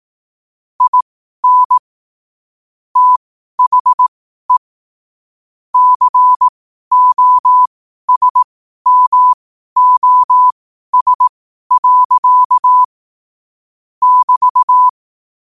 Morse code, heard as a keyed tone, about 18 wpm, spaced as for 12 wpm, 1000 Hz. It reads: IN THE COSMOS. <BT>
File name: qotdfiles / 12wpm